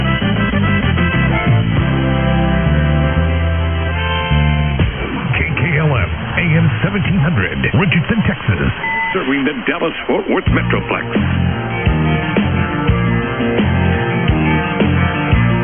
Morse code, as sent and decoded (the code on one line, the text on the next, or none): none